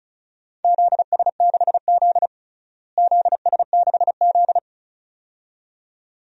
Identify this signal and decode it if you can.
Morse code, heard as a keyed tone, about 36 wpm, 700 Hz. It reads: ZS6Z ZS6Z
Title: DXCC-100 Call Signs - 4 Characters - Repeated 2x - 35wpm